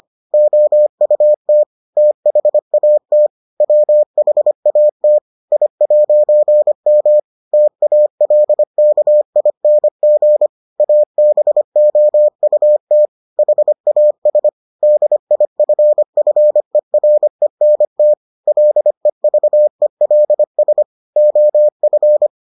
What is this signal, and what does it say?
Morse code, heard as a keyed tone, about 25 wpm, 600 Hz. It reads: OUT THAT WHAT I'M TALKING ABOUT HAS DIFFERENT LEVELS OF